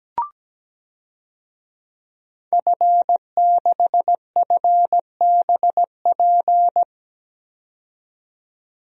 Morse code, heard as a keyed tone, about 17 wpm, 700 Hz. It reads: F6FBP
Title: DXCC-100 Call Signs - 5 Characters - 17wpm